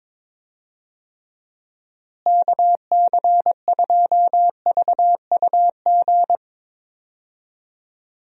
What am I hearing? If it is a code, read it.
Morse code, heard as a keyed tone, about 21 wpm, 700 Hz. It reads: KC2VUG
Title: USA Call Signs - 6 Characters - Repeated 3x - 22wpm